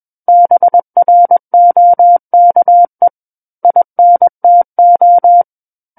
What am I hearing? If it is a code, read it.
Morse code, heard as a keyed tone, about 21 wpm, 700 Hz. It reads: BROKE INTO
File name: Book One / 12-Chapter 12 - 21 WPM